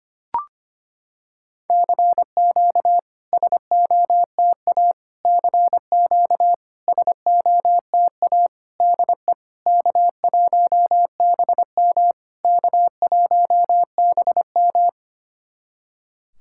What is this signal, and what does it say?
Morse code, heard as a keyed tone, about 25 wpm, 690 Hz. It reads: CQ SOTA CQ SOTA DE K1BM K1BM
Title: SOTA QSO Practice 25wpm